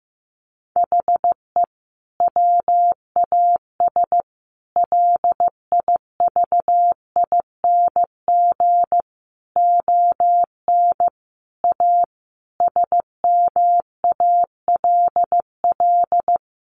Morse code, heard as a keyed tone, about 15 wpm, 700 Hz. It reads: HE WAS LIVING ON A SMALL
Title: Sentences from Top 1200 Words 15wpm